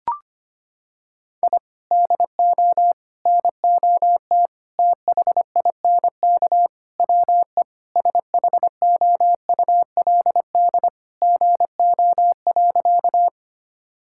Morse code, heard as a keyed tone, about 25 wpm, 700 Hz. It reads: I DO NOT THINK WE SHOULD GO.
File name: practice / 25wpm / Sentences with 4-Word Phrases - 25wpm